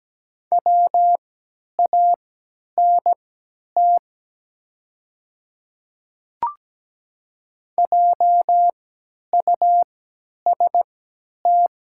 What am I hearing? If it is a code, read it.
Morse code, heard as a keyed tone, about 17 wpm, spaced as for 10 wpm, 690 Hz. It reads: WANT JUST
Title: 3x Character Spacing - Top 300 Words - 17wpm